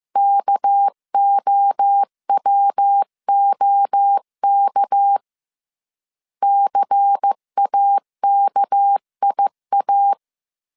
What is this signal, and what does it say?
Morse code, heard as a keyed tone, about 15 wpm, 800 Hz. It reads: KOWOK CAKIA
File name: CW / DET / 1 6 1500C S B 14wpm 2958